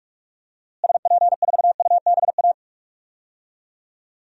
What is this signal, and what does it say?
Morse code, heard as a keyed tone, about 42 wpm, 700 Hz. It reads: SP4UBA
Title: DXCC-100 Call Signs - 6 Characters - Repeated 2x - 45wpm